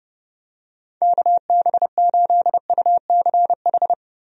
Morse code, heard as a keyed tone, about 30 wpm, 700 Hz. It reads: KB8UCH
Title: USA Call Signs - 6 Characters - 30wpm